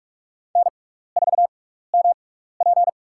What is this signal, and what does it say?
Morse code, heard as a keyed tone, about 46 wpm, spaced as for 15 wpm, 700 Hz. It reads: N4MP
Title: USA Call Signs - 4 Characters - 6x Character Spacing - 45wpm